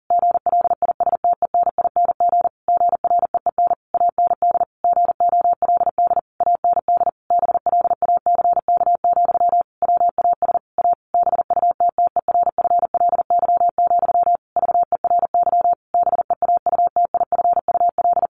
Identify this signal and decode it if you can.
Morse code, heard as a keyed tone, about 40 wpm, 700 Hz. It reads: GLISTENING GREEN AND GOLD AND BLACK, WAS A BUTTERFLY, VERY BEAUTIFUL
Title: A Sound Of Thunder 40wpm